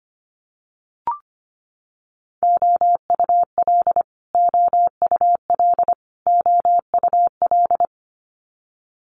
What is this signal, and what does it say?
Morse code, heard as a keyed tone, about 25 wpm, 700 Hz. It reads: OUL OUL OUL
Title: Multi-letter Phonemes Repeated 3x 25wpm